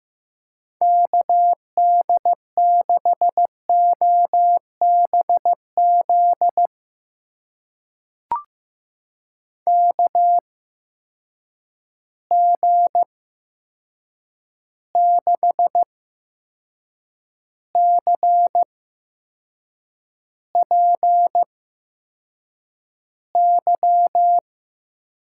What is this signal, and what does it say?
Morse code, heard as a keyed tone, about 15 wpm, spaced as for 6 wpm, 700 Hz. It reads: KD6OBZ KG6CPY
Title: USA Call Signs - 6 Characters - 8x Character Spacing - 15wpm